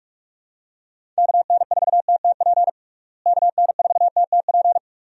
Morse code, heard as a keyed tone, about 43 wpm, 700 Hz. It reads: KN4TTP KN4TTP
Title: USA Call Signs - 6 Characters - Repeated 2x - 45wpm